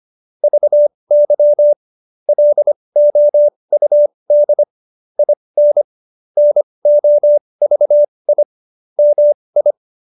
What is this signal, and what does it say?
Morse code, heard as a keyed tone, about 25 wpm, spaced as for 20 wpm, 590 Hz. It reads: VY LOUD IN NOVI MI